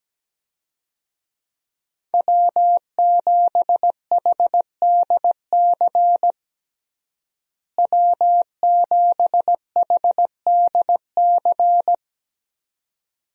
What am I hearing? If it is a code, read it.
Morse code, heard as a keyed tone, about 17 wpm, 700 Hz. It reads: W7HDC W7HDC
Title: USA Call Signs - 5 Characters - Repeated 2x - 17wpm